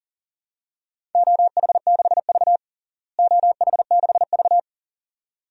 Morse code, heard as a keyed tone, about 40 wpm, 700 Hz. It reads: OH6V OH6V
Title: DXCC-100 Call Signs - 4 Characters - Repeated 2x - 40wpm